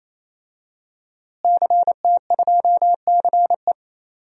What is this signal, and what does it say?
Morse code, heard as a keyed tone, about 27 wpm, 700 Hz. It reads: CT2CE
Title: DXCC-100 Call Signs - 5 Characters - Repeated 3x - 28wpm